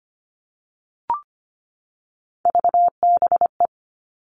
Morse code, heard as a keyed tone, about 25 wpm, 700 Hz. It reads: VBE
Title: All Cut Numbers - 3-Digit 25wpm